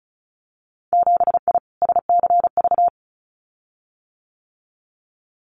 Morse code, heard as a keyed tone, about 36 wpm, 700 Hz. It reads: 7I SCV
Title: Winter Field Day Exchanges Encore 35wpm